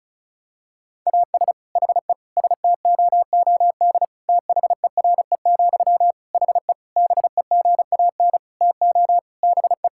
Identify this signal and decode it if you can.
Morse code, heard as a keyed tone, about 36 wpm, 700 Hz. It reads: AS HE STOOD THERE, HE BEGAN TO BE
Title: Sentences from Top 1600 Words 35wpm